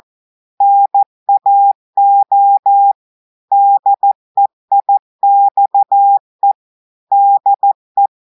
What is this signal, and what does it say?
Morse code, heard as a keyed tone, about 14 wpm, 800 Hz. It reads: NAO DEIXE DE
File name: audios / portugais / portugais texte en clair 2 vitesse 14 mots minutes